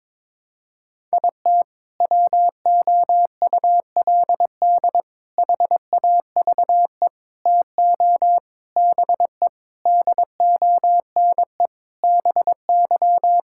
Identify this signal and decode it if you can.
Morse code, heard as a keyed tone, about 22 wpm, 700 Hz. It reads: IT WOULD HAVE TO BE DONE BY
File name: practice / 22wpm / Sentences with 4-Word Phrases - 22wpm